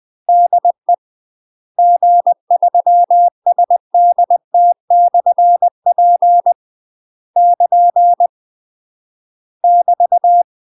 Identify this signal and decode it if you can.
Morse code, heard as a keyed tone, about 20 wpm, 700 Hz. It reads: DE G3SDT/P <KN> <BT>